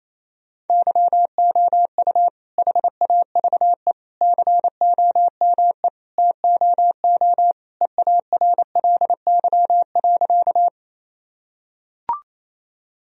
Morse code, heard as a keyed tone, about 28 wpm, 700 Hz. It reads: YOU HAVE COME TOO EARLY.
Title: Sentences from Top 1000 Words - Review Part 2 - 28wpm